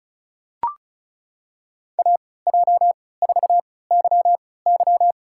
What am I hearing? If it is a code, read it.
Morse code, heard as a keyed tone, about 36 wpm, spaced as for 20 wpm, 700 Hz. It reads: AJ4YY
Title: USA Call Signs - 5 Characters - 3x Character Spacing - 35wpm